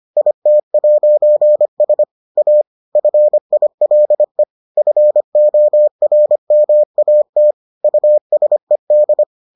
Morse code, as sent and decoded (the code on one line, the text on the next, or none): .. - .----. ... / .- / ..-. .. .-.. / ..-. --- .-. -- .- - / ..- ... -..
IT'S A FILE FORMAT USED